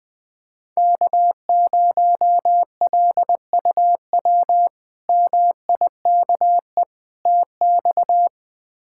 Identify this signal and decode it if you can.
Morse code, heard as a keyed tone, about 20 wpm, 700 Hz. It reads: K0LUW MIKE TX
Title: K1USN SST QSO Practice - 20wpm